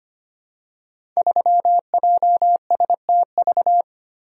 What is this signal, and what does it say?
Morse code, heard as a keyed tone, about 25 wpm, 700 Hz. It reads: 3JSTV
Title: Code Groups - 5 Letter:Numbers - 25wpm